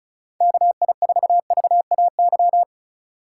Morse code, heard as a keyed tone, about 36 wpm, 700 Hz. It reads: KI4VAY
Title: USA Call Signs - 6 Characters - 4x Character Spacing - 35wpm